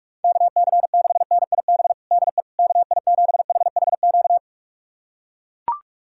Morse code, heard as a keyed tone, about 42 wpm, 700 Hz. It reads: KC6NIB DE KI7HHQ
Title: USA Callsign DE Callsign - 6 Characters - 45wpm